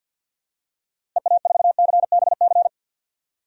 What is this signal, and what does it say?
Morse code, heard as a keyed tone, about 51 wpm, 700 Hz. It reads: EA4CBC
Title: DXCC-100 Call Signs - 6 Characters - Repeated 2x - 50wpm